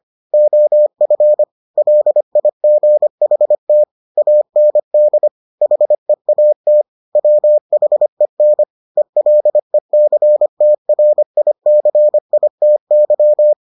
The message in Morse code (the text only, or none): OF LIGHT AND HEAT WHEN ELECTRICITY